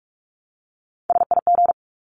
Morse code, heard as a keyed tone, about 44 wpm, 700 Hz. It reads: SIG
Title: Abbreviations 45wpm